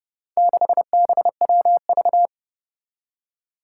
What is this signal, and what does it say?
Morse code, heard as a keyed tone, about 30 wpm, 700 Hz. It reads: 6BWV
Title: Four Challenging Letter-Number Combinations 30wpm